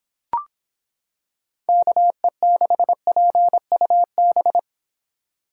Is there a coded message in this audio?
Yes: Morse code from 1.7 to 4.6 s, heard as a keyed tone, about 26 wpm, 700 Hz. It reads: KE6PUB